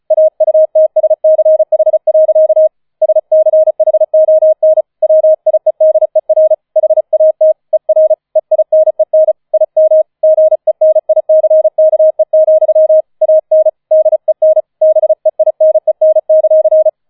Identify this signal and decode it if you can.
Morse code, heard as a keyed tone, about 34 wpm, 630 Hz. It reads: AUTSCH. SCHON WIEDER HAT ER EINEN IM GENICKE, AN DEN BEINEN;
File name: 005 Fünfter Streich 35wpm